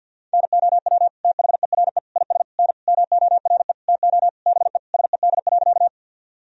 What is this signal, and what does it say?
Morse code, heard as a keyed tone, about 53 wpm, 700 Hz. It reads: NOW THERE IS N MORE TO BE SED.
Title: Sentences from Top 100 Words and Abbreviations 50wpm